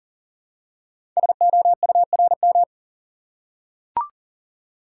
Morse code, heard as a keyed tone, about 39 wpm, 690 Hz. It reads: SOURM
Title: Code Groups - 5 Letters - 40wpm